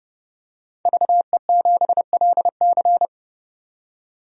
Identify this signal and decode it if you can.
Morse code, heard as a keyed tone, about 28 wpm, 690 Hz. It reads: VE7LC
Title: DXCC-100 Call Signs - 5 Characters - 30wpm